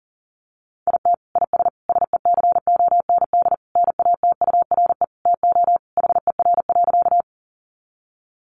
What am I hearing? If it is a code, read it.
Morse code, heard as a keyed tone, about 40 wpm, 700 Hz. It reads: IT IS SECOND NATURE TO HER.
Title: Sentences from Top 600 Words 40wpm